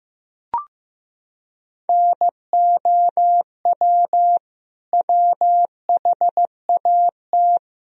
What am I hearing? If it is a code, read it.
Morse code, heard as a keyed tone, about 15 wpm, 700 Hz. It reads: NOW WHAT
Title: Sentences from Top 100 Words and Abbreviations 15wpm